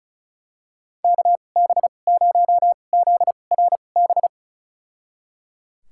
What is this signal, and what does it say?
Morse code, heard as a keyed tone, about 34 wpm, spaced as for 25 wpm, 700 Hz. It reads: KB0ZRB
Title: USA Call Signs - 6 Characters - 2x Character Spacing - 35wpm